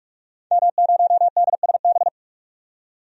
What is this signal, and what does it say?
Morse code, heard as a keyed tone, about 44 wpm, 700 Hz. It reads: M0DSB
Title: DXCC-100 Call Signs - 5 Characters - 45wpm